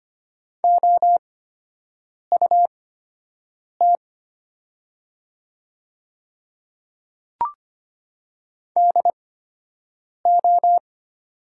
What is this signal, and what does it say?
Morse code, heard as a keyed tone, about 25 wpm, spaced as for 7 wpm, 700 Hz. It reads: OUT DO
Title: 8x Character Spacing - Top 100 Words - 25wpm